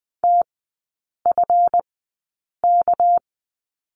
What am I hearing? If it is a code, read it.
Morse code, heard as a keyed tone, about 20 wpm, spaced as for 8 wpm, 700 Hz. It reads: TFK
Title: 198 Lesson 32e - Letter k - Three Characters 20wpm